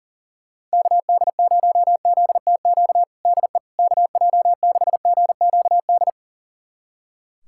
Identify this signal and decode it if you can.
Morse code, heard as a keyed tone, about 40 wpm, 700 Hz. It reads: KD0ZTQ DE KJ6GQD
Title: USA Callsign DE Callsign 40wpm